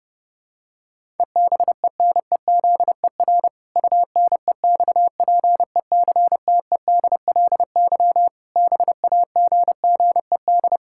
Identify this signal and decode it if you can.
Morse code, heard as a keyed tone, about 32 wpm, 700 Hz. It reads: EBENEZER UNEXPECTEDLY BAGGED
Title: Pangram Challenge v2 - 30wpm